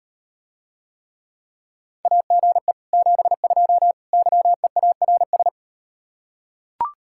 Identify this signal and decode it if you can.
Morse code, heard as a keyed tone, about 38 wpm, 700 Hz. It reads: AGE 72 YEARS